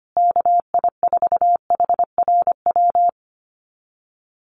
Morse code, heard as a keyed tone, about 25 wpm, 700 Hz. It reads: KI4HRW
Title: USA Call Signs Repeated 3x 25wpm